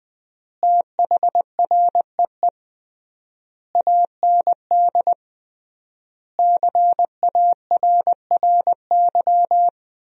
Morse code, heard as a keyed tone, about 20 wpm, 700 Hz. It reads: THREE AND CARRY
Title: 3x Word-Spacing - Sentences from Top 400 Words - 20wpm